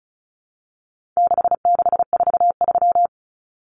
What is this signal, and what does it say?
Morse code, heard as a keyed tone, about 34 wpm, 700 Hz. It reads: 6643
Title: Common Cut Numbers - 4-Digit 35wpm